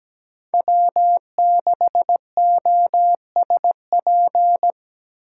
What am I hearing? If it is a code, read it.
Morse code, heard as a keyed tone, about 17 wpm, 700 Hz. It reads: W6OSP
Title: K1USN SST QSO Practice - 17wpm